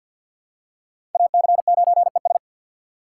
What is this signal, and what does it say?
Morse code, heard as a keyed tone, about 48 wpm, 700 Hz. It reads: AC9ES
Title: USA Call Signs - 5 Characters - 8x Character Spacing - 50wpm